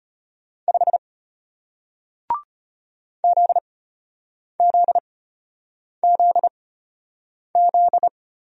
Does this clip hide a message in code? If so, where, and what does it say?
Yes: Morse code from 0.7 to 8.1 s, heard as a keyed tone, about 33 wpm, spaced as for 8 wpm, 700 Hz. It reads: 5 ZZZZ